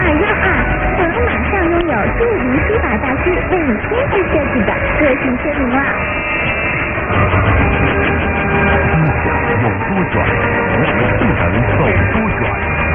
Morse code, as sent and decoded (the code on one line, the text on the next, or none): none